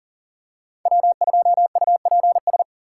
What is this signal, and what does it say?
Morse code, heard as a keyed tone, about 40 wpm, 700 Hz. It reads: W2UPS
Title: USA Call Signs - 5 Characters - 3x Character Spacing - 40wpm